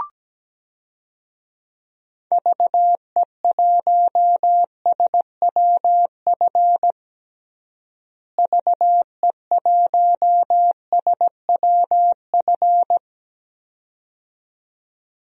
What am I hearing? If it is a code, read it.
Morse code, heard as a keyed tone, about 17 wpm, 700 Hz. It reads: VE1SWF VE1SWF
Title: DXCC-100 Call Signs - 6 Characters - Repeated 2x - 17wpm